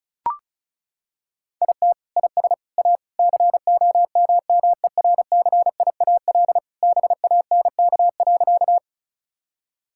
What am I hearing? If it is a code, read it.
Morse code, heard as a keyed tone, about 36 wpm, 700 Hz. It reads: IT IS A COMMERCIAL BANK.
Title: Sentences from Top 1000 Words - Review Part 6 - 35wpm